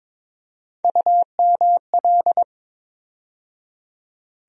Morse code, heard as a keyed tone, about 22 wpm, 700 Hz. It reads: UML
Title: USA Call Sign Postfixes - 3 Letters 22wpm